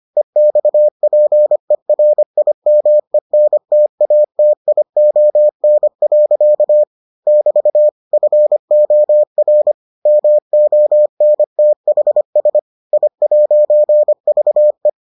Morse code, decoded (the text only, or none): EXPERIMENTATION. <BT> FOR MONTHS I'VE